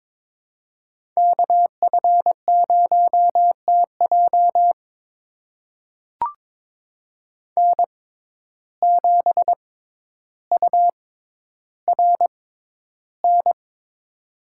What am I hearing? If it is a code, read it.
Morse code, heard as a keyed tone, about 22 wpm, spaced as for 11 wpm, 700 Hz. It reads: KF0TJ N7URN
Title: USA Call Signs - 5 Characters - 6x Character Spacing - 22wpm